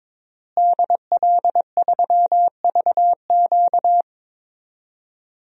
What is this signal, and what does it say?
Morse code, heard as a keyed tone, about 21 wpm, 700 Hz. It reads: DL3VQ
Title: DXCC-100 Call Signs - 5 Characters - Repeated 2x - 22wpm